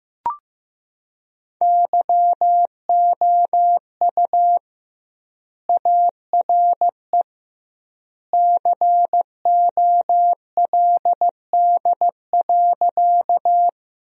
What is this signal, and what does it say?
Morse code, heard as a keyed tone, about 15 wpm, 700 Hz. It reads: YOU ARE COLD.
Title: 2x Word-Spacing - Sentences from Top 400 Words - 15wpm